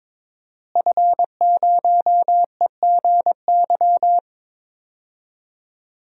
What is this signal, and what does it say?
Morse code, heard as a keyed tone, about 22 wpm, 700 Hz. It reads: F0EGY
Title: DXCC-100 Call Signs - 5 Characters - Repeated 2x - 22wpm